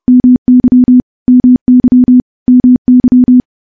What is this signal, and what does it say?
Morse code, heard as a keyed tone, about 30 wpm, 250 Hz. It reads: MY MY MY
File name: morse / morse-training-30WPM